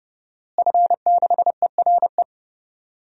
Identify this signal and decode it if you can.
Morse code, heard as a keyed tone, about 32 wpm, 700 Hz. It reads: F6ERE
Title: DXCC-100 Call Signs - 5 Characters - Repeated 3x - 30wpm